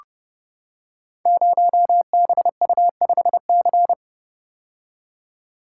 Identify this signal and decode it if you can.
Morse code, heard as a keyed tone, about 30 wpm, 700 Hz. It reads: 0BU5C
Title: Code Groups - 5 Letter:Numbers - 30wpm